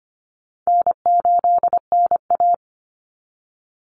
Morse code, heard as a keyed tone, about 25 wpm, 700 Hz. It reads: N8NA
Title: Call Signs - 25wpm